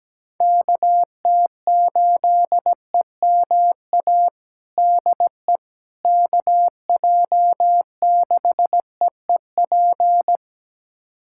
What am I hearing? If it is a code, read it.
Morse code, heard as a keyed tone, about 17 wpm, 700 Hz. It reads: KT8EMA DE KJ6EEP
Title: USA Callsign DE Callsign 17wpm